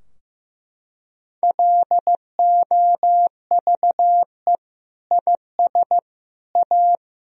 Morse code, heard as a keyed tone, about 15 wpm, 700 Hz. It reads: LOVE IS A
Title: Sentences from Top 1000 Words - Review Part 4 - 15wpm